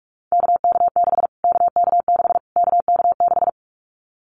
Morse code, heard as a keyed tone, about 44 wpm, 700 Hz. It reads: KK6 KK6 KK6
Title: Callsign Prefixes for Top 25 DXCC - Repeated 3x - 45wpm